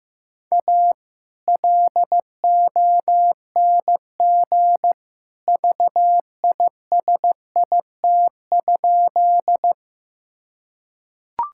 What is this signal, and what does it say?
Morse code, heard as a keyed tone, about 15 wpm, 700 Hz. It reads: A LONG VISIT?
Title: Sentences from Top 1000 Words - Review Part 6 - 15wpm